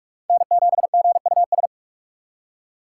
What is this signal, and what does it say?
Morse code, heard as a keyed tone, about 47 wpm, 700 Hz. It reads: N7GUS
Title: USA Call Signs - 5 Characters - 6x Character Spacing - 45wpm